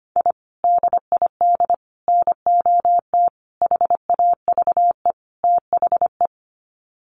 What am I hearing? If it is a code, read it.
Morse code, heard as a keyed tone, about 25 wpm, 700 Hz. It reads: I DID NOT HAVE THE
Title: Sets of 5 Words Encore 25wpm